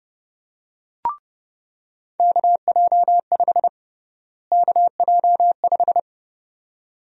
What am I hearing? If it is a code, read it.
Morse code, heard as a keyed tone, about 29 wpm, 700 Hz. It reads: KJ5 KJ5
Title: USA Call Sign Prefixes - Repeated 2x - 30wpm